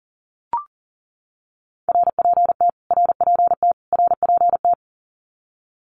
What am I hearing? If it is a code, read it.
Morse code, heard as a keyed tone, about 40 wpm, 700 Hz. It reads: RPT RPT RPT
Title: Abbreviations Repeated 3x - 40wpm